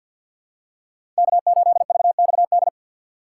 Morse code, heard as a keyed tone, about 52 wpm, 700 Hz. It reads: K8VXD
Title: USA Call Signs - 5 Characters - 2x Character Spacing - 50wpm